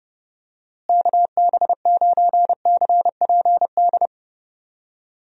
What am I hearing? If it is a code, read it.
Morse code, heard as a keyed tone, about 31 wpm, 700 Hz. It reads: KB9CPD
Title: USA Call Signs - 6 Characters - Repeated 3x - 30wpm